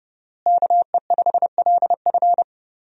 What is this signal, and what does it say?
Morse code, heard as a keyed tone, about 30 wpm, 700 Hz. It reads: KE5LF
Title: USA Call Signs - 5 Characters - Repeated 2x - 30wpm